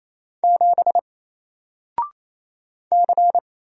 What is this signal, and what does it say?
Morse code, heard as a keyed tone, about 27 wpm, 700 Hz. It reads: 7 C
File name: Single-Letter-Number-28wpm